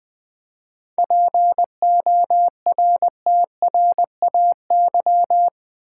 Morse code, heard as a keyed tone, about 20 wpm, 700 Hz. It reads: PORTRAY
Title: 7 Letter Words - First 3 Letters Given - 20wpm